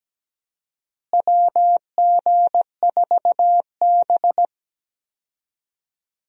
Morse code, heard as a keyed tone, about 17 wpm, 700 Hz. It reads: WG4B
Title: USA Call Signs - 4 Characters - Repeated 2x - 17wpm